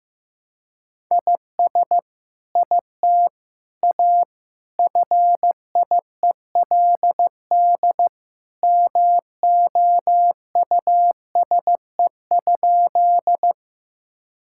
Morse code, heard as a keyed tone, about 15 wpm, 700 Hz. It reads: IS IT A FIELD MOUSE?
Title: Sentences from Top 1000 Words - Review Part 5 - 15wpm